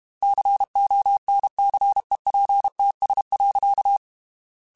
Morse code, heard as a keyed tone, about 32 wpm, 800 Hz. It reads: CONCEPTS.